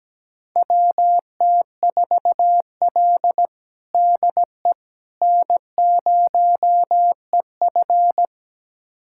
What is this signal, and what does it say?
Morse code, heard as a keyed tone, about 17 wpm, 700 Hz. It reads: WT4L DE N0EF
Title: USA Callsign DE Callsign - 4 Characters - 17wpm